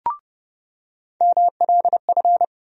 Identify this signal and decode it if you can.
Morse code, heard as a keyed tone, about 30 wpm, 700 Hz. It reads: MLF
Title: USA Call Sign Postfixes - 3 Letters 30wpm